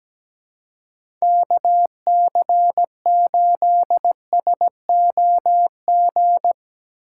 Morse code, heard as a keyed tone, about 17 wpm, 700 Hz. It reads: KC8SOG
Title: USA Call Signs - 2x Character Spacing - 17wpm